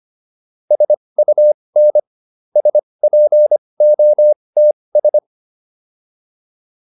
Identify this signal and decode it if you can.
Morse code, heard as a keyed tone, about 25 wpm, spaced as for 20 wpm, 600 Hz. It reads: SUN SPOTS